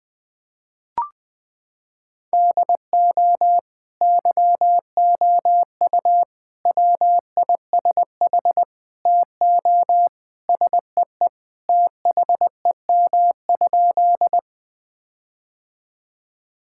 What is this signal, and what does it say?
Morse code, heard as a keyed tone, about 20 wpm, 700 Hz. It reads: DO YOU WISH TO SEE THEM?
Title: Sentences from Top 1600 Words 20wpm